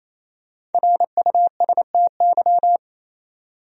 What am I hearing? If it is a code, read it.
Morse code, heard as a keyed tone, about 28 wpm, 700 Hz. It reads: RUSTY
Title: USA Common Names - Max 5 Letters - 28wpm